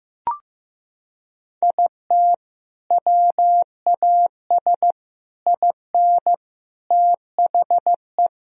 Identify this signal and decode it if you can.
Morse code, heard as a keyed tone, about 15 wpm, 700 Hz. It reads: IT WAS IN THE
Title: Sentences from Top 1000 Words - Review Part 1 - 15wpm